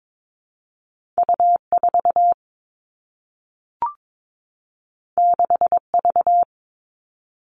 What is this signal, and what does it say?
Morse code, heard as a keyed tone, about 22 wpm, 700 Hz. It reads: U 4 6 V
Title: Two Challenging Letter-Number Combinations 22wpm